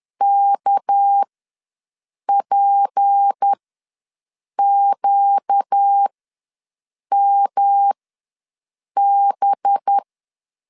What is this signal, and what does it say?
Morse code, heard as a keyed tone, about 10 wpm, spaced as for 6 wpm, 800 Hz. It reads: KPQMB
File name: LEZIONI / 6 PKX0 / 6 PK ws 10wpm 3745